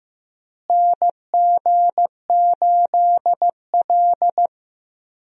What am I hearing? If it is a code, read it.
Morse code, heard as a keyed tone, about 16 wpm, 700 Hz. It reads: NG8L
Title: USA Call Signs - 4 Characters - Repeated 3x - 15wpm